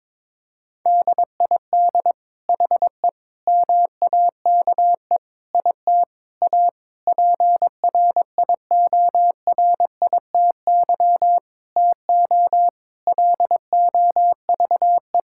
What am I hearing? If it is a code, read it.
Morse code, heard as a keyed tone, about 22 wpm, 700 Hz. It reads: DID HE MAKE IT A PRIORITY TO LOVE
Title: Sentences from Top 1600 Words 22wpm